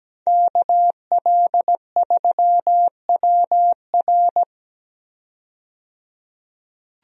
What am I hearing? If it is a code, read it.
Morse code, heard as a keyed tone, about 17 wpm, 700 Hz. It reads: KL3WR
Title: USA Call Signs - 5 Characters - 17wpm